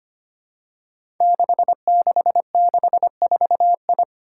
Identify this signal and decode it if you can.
Morse code, heard as a keyed tone, about 25 wpm, 700 Hz. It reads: 6664I
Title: Five Challenging Letter-Number Combinations 25wpm